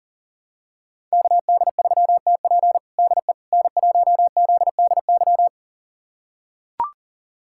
Morse code, heard as a keyed tone, about 40 wpm, 700 Hz. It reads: KD3TP DE N1ZDY
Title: USA Callsign DE Callsign - 5 Characters - 40wpm